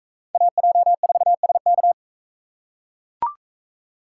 Morse code, heard as a keyed tone, about 41 wpm, 700 Hz. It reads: AJ4SK